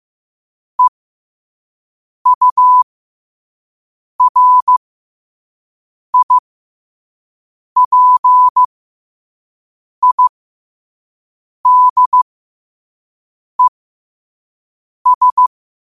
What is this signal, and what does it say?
Morse code, heard as a keyed tone, about 15 wpm, spaced as for 5 wpm, 1000 Hz. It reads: EURIPIDES
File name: qotdfiles / 5wpm